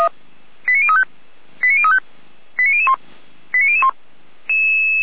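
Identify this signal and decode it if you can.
Five-tone selective call (ZVEI), heard as quick runs of five tones, tones of about 80 ms. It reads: 89035 89035 89001 89001
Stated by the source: ZVEI five-tone selective call 89117 received via SDR.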